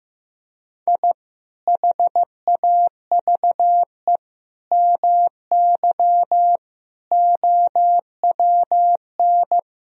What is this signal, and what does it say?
Morse code, heard as a keyed tone, about 15 wpm, 700 Hz. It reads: I HAVE MY OWN
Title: Sentences from Top 1000 Words - Review Part 3 - 15wpm